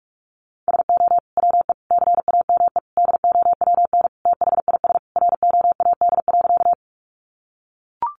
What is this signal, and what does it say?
Morse code, heard as a keyed tone, about 44 wpm, 700 Hz. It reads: SO WE CAME DOWN THIS ROAD.
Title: Sentences from Top 400 Words 45wpm